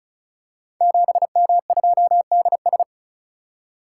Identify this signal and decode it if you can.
Morse code, heard as a keyed tone, about 36 wpm, 690 Hz. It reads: 7M2DS
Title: Code Groups - 5 Letter:Numbers - 35wpm